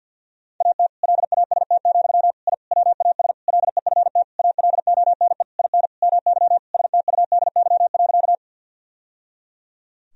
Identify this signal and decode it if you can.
Morse code, heard as a keyed tone, about 50 wpm, 700 Hz. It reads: AT LAST, I WAS LEFT ALONE IN MY STUDY.
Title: Sentences from Top 1000 Words - Review Part 1 - 50wpm